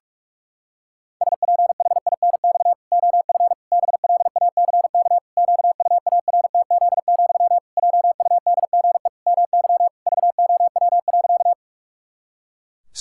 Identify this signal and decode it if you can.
Morse code, heard as a keyed tone, about 46 wpm, 700 Hz. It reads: SPHINX OF BLACK QUARTZ, JUDGE MY VOW.
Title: Pangram Challenge v2 - 45wpm